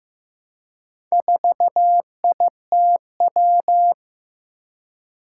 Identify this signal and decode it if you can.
Morse code, heard as a keyed tone, about 15 wpm, 700 Hz. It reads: 4ITW
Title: Four Challenging Letter-Number Combinations 15wpm